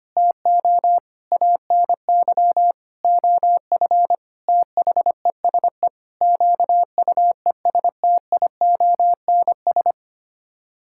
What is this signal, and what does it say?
Morse code, heard as a keyed tone, about 25 wpm, 700 Hz. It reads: TO ANY OF THESE QUESTIONS
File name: Sets of 9 Words - 25wpm